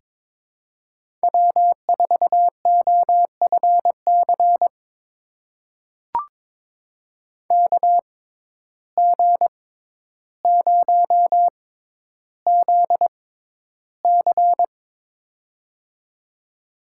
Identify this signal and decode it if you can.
Morse code, heard as a keyed tone, about 22 wpm, spaced as for 11 wpm, 700 Hz. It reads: W4OFC KG0ZC
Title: USA Call Signs - 5 Characters - 6x Character Spacing - 22wpm